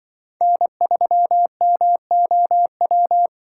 Morse code, heard as a keyed tone, about 25 wpm, 700 Hz. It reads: N3MOW
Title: Speed Racing - USA 1x3 Call Signs 20wpm